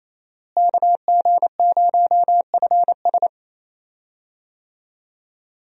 Morse code, heard as a keyed tone, about 28 wpm, 700 Hz. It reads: KG0FS
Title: USA Call Signs - 5 Characters - Repeated 2x - 28wpm